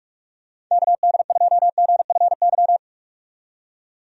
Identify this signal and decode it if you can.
Morse code, heard as a keyed tone, about 44 wpm, 700 Hz. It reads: KD2GFY
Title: USA Call Signs - 6 Characters - 45wpm